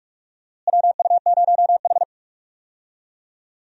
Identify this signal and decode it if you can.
Morse code, heard as a keyed tone, about 47 wpm, 700 Hz. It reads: WU0H
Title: USA Call Signs - 4 Characters - Repeated 2x - 45wpm